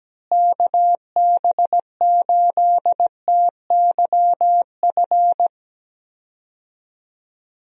Morse code, heard as a keyed tone, about 17 wpm, 700 Hz. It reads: KB8TYF
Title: USA Call Signs - 6 Characters - Repeated 2x - 17wpm